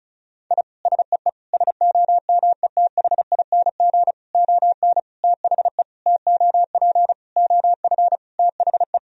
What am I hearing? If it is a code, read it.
Morse code, heard as a keyed tone, about 34 wpm, 700 Hz. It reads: I SEE SOMETHING ON THE TOP OF THE
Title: Sentences from Top 1000 Words - Review Part 2 - 35wpm